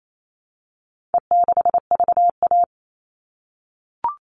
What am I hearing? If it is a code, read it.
Morse code, heard as a keyed tone, about 28 wpm, 700 Hz. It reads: E6VA
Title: All Cut Numbers - 4-Digit 28wpm